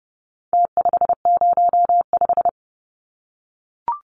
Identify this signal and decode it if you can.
Morse code, heard as a keyed tone, about 30 wpm, 700 Hz. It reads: T505
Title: Numbers - Contest Style - 4-Digit 30wpm